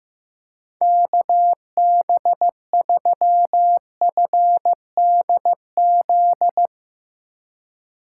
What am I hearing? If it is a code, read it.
Morse code, heard as a keyed tone, about 15 wpm, 700 Hz. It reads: KB3FDZ
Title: USA Call Signs - 6 Characters - 2x Character Spacing - 15wpm